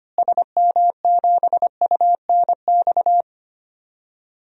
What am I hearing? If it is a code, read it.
Morse code, heard as a keyed tone, about 25 wpm, 700 Hz. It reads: SM7UNX
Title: DXCC-100 Call Signs - 6 Characters - Repeated 3x - 25wpm